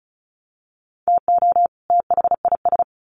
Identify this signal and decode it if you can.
Morse code, heard as a keyed tone, about 34 wpm, 700 Hz. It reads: TO THIS
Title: Sets of 2 Words 35wpm